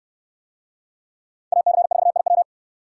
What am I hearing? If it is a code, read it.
Morse code, heard as a keyed tone, about 64 wpm, 700 Hz. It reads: AC4IW